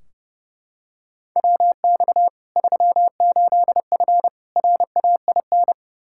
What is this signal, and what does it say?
Morse code, heard as a keyed tone, about 32 wpm, 700 Hz. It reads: WX 38F RAIN
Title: Practice QSO - Exchange 2 - Info - 30wpm